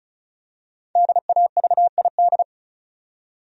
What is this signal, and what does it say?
Morse code, heard as a keyed tone, about 36 wpm, 700 Hz. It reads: DAVID
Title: USA Common Names - Max 5 Letters - 35wpm